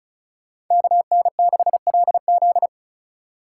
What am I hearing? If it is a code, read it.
Morse code, heard as a keyed tone, about 36 wpm, 700 Hz. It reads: KN6LZ
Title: USA Call Signs - 5 Characters - 3x Character Spacing - 35wpm